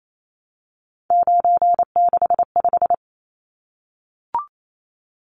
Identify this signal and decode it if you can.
Morse code, heard as a keyed tone, about 28 wpm, 700 Hz. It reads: 965
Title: Numbers - 3-Digit 28wpm